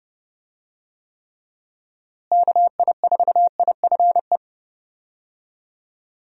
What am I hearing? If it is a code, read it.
Morse code, heard as a keyed tone, about 29 wpm, 700 Hz. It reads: KI4IFE
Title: USA Call Signs - 6 Characters - Repeated 3x - 30wpm